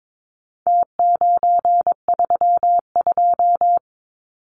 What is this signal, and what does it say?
Morse code, heard as a keyed tone, about 22 wpm, 700 Hz. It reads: T932
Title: Numbers - Contest Style - 4-Digit 22wpm